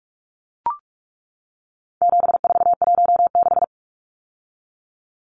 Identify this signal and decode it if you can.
Morse code, heard as a keyed tone, about 44 wpm, 700 Hz. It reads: 7416
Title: Numbers - 4-Digit 45wpm